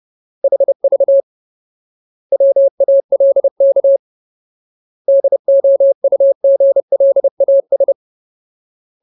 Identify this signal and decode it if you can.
Morse code, heard as a keyed tone, about 30 wpm, 550 Hz. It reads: HV WALK DOUGLAS
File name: Ultimate Word List / 30WPM Ultimate